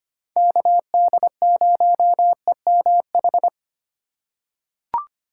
Morse code, heard as a keyed tone, about 25 wpm, 700 Hz. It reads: KD0EMH
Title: USA Call Signs - 4x Character Spacing - 25wpm